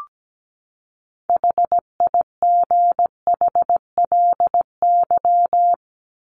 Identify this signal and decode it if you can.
Morse code, heard as a keyed tone, about 17 wpm, 700 Hz. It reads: HIGHLY
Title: Top 1001-1250 Words 17wpm